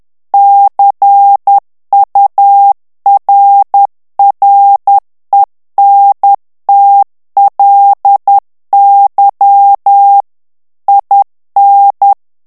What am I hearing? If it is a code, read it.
Morse code, heard as a keyed tone, about 10 wpm, 800 Hz. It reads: CURRENTLY IN
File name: ANGLAIS texte clair 3 vitesse 10 mots minute